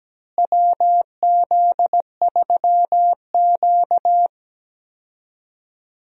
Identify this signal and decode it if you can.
Morse code, heard as a keyed tone, about 17 wpm, 700 Hz. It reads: WZ3Q
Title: USA Call Signs - 4 Characters - Repeated 2x - 17wpm